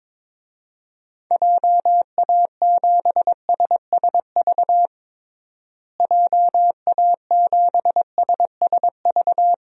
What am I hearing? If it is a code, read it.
Morse code, heard as a keyed tone, about 21 wpm, 700 Hz. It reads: JA7SSV JA7SSV
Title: DXCC-100 Call Signs - 6 Characters - Repeated 3x - 22wpm